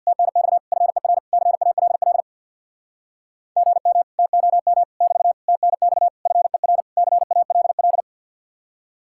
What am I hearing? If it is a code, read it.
Morse code, heard as a keyed tone, about 62 wpm, 700 Hz. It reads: TNX FER CALL GM TOM <BT> TNX FER CALL
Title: Speed-Racing Common QSO Phrases - 50wpm